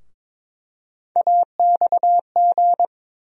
Morse code, heard as a keyed tone, about 22 wpm, 700 Hz. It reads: AXG
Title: USA Call Sign Postfixes - 3 Letters 22wpm